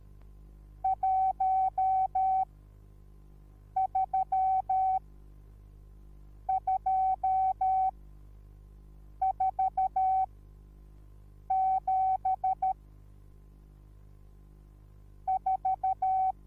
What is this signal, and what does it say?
Morse code, heard as a keyed tone, about 13 wpm, 760 Hz. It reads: 1 3 2 4 7 4
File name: Onlinelehrgang / morsen / DJ4UF Morsen21